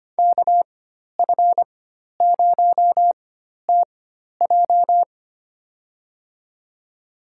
Morse code, heard as a keyed tone, about 25 wpm, spaced as for 12 wpm, 700 Hz. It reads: KF0TJ
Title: USA Call Signs - 5 Characters - 4x Character Spacing - 25wpm